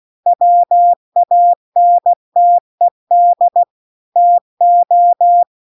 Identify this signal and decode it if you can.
Morse code, heard as a keyed tone, about 16 wpm, 700 Hz. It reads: WANTED TO